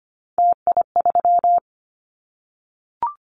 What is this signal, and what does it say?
Morse code, heard as a keyed tone, about 25 wpm, 700 Hz. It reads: TI3
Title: Callsign Prefixes for Top 100 DXCC - 25wpm